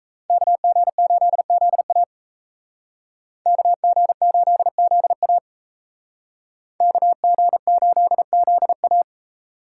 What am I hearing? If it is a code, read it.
Morse code, heard as a keyed tone, about 32 wpm, 700 Hz. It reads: KG8ZA KG8ZA KG8ZA
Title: Speed Racing - USA 2x2 Call Signs 28wpm